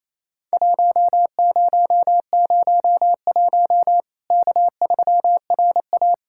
Morse code, heard as a keyed tone, about 28 wpm, 700 Hz. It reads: 1001 K3RA
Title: Worked All Europe DX Contest - QTC Exchange - 28wpm